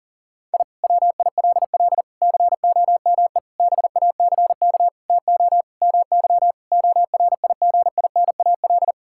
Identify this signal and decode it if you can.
Morse code, heard as a keyed tone, about 40 wpm, 700 Hz. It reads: I WILL COME BACK TO MY ORIGINAL
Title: Sentences from Top 1000 Words - Review Part 6 - 40wpm